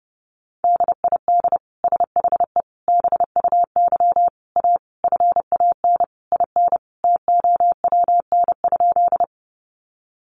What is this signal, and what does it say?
Morse code, heard as a keyed tone, about 30 wpm, 700 Hz. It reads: DID SHE BUY A FAN IN TOWN?
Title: Sentences from Top 1300 Words 30wpm